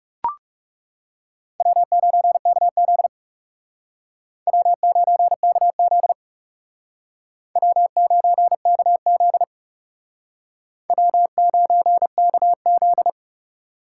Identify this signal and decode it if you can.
Morse code, heard as a keyed tone, about 38 wpm, 700 Hz. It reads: W9KZ W9KZ W9KZ W9KZ